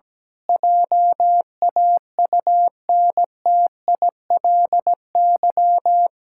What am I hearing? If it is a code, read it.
Morse code, heard as a keyed tone, about 17 wpm, 700 Hz. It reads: JAUNTILY
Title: Pangram Challenge v2 - 17wpm